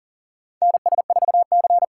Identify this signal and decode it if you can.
Morse code, heard as a keyed tone, about 38 wpm, 700 Hz. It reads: NS4C
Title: USA Call Signs - 4 Characters - 6x Character Spacing - 40wpm